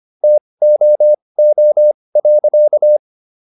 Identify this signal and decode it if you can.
Morse code, heard as a keyed tone, about 25 wpm, spaced as for 20 wpm, 600 Hz. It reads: TOO.